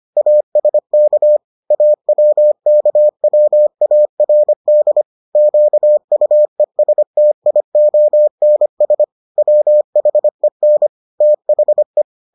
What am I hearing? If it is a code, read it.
Morse code, heard as a keyed tone, about 25 wpm, 600 Hz. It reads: ASK AWKWARD QUESTIONS WHEN THE